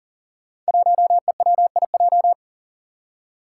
Morse code, heard as a keyed tone, about 40 wpm, 700 Hz. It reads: 1EWIJ
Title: Five Challenging Letter-Number Combinations 40wpm